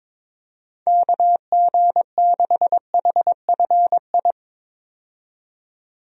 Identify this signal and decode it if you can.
Morse code, heard as a keyed tone, about 22 wpm, 700 Hz. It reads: KG6HFI
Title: USA Call Signs - 6 Characters - 22wpm